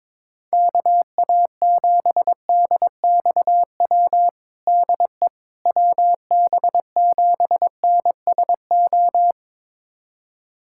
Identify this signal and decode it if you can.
Morse code, heard as a keyed tone, about 22 wpm, 700 Hz. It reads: KA7DXW DE WB7NSO
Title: USA Callsign DE Callsign 22wpm